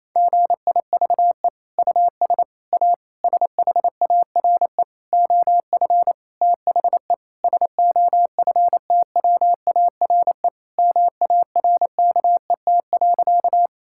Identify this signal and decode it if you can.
Morse code, heard as a keyed tone, about 28 wpm, 700 Hz. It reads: GIVE US A SHARE OF THE SOFTWARE MARKET.
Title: Sentences from Top 1000 Words - Review Part 6 - 28wpm